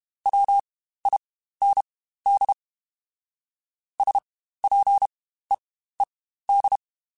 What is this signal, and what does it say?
Morse code, heard as a keyed tone, about 32 wpm, spaced as for 15 wpm, 800 Hz. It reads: WIND SPEED
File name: Spaced Characters / Spaced Characters-mp3s / Weather 5 5 sp @30